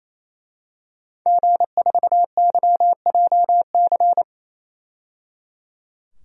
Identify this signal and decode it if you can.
Morse code, heard as a keyed tone, about 28 wpm, 700 Hz. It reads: G4YJC
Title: DXCC-100 Call Signs - 5 Characters - 28wpm